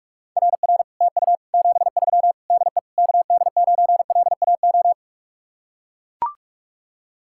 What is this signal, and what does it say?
Morse code, heard as a keyed tone, about 44 wpm, 700 Hz. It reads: RR TU 73 DE KD9LAO